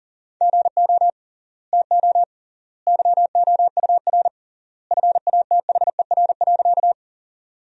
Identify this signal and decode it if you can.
Morse code, heard as a keyed tone, about 40 wpm, 700 Hz. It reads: GO TO YOUR FATHER.
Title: 3x Word-Spacing - Sentences from Top 300 Words - 40wpm